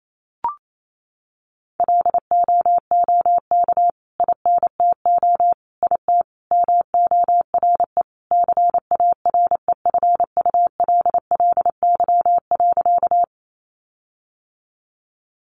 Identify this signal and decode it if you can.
Morse code, heard as a keyed tone, about 28 wpm, 700 Hz. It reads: LOOK INTO IT MORE CAREFULLY.
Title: Sentences from Top 1300 Words 28wpm